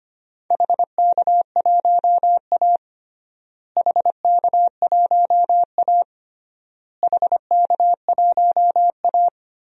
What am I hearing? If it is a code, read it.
Morse code, heard as a keyed tone, about 25 wpm, 700 Hz. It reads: HK1A HK1A HK1A
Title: DXCC-100 Call Signs - 4 Characters - Repeated 3x - 25wpm